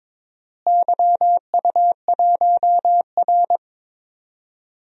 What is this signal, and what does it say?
Morse code, heard as a keyed tone, about 22 wpm, 700 Hz. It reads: YU1R
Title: DXCC-100 Call Signs - 4 Characters - Repeated 2x - 22wpm